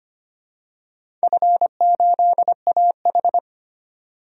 Morse code, heard as a keyed tone, about 25 wpm, 700 Hz. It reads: F8AH
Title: DXCC-100 Call Signs - 4 Characters - 25wpm